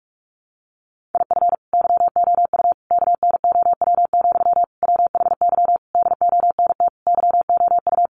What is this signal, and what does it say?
Morse code, heard as a keyed tone, about 44 wpm, 700 Hz. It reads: IF YOU KNOW, WHY DONT YOU
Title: Sentences from Top 300 Words 45wpm